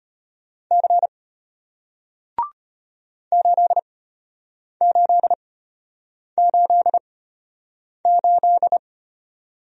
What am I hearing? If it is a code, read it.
Morse code, heard as a keyed tone, about 33 wpm, spaced as for 8 wpm, 700 Hz. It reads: C 8888